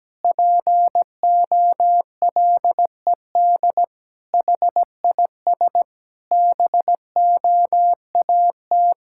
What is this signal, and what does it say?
Morse code, heard as a keyed tone, about 17 wpm, 700 Hz. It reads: POLED HIS BOAT
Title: Pangram Challenge v2 - 17wpm